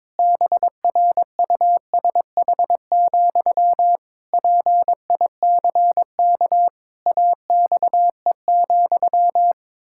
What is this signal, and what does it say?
Morse code, heard as a keyed tone, about 22 wpm, 700 Hz. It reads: BRUSH, PICK AXE,
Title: Pangram Challenge v2 - 22wpm